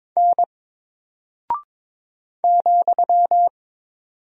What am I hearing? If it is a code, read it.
Morse code, heard as a keyed tone, about 22 wpm, 700 Hz. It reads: N ,
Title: Single Character (Letter:Number:Punctuation:Prosign) Alt 22wpm